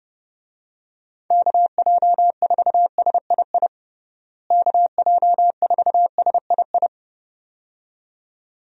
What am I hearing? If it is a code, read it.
Morse code, heard as a keyed tone, about 32 wpm, 700 Hz. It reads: KJ4SII KJ4SII
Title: USA Call Signs - 6 Characters - Repeated 2x - 30wpm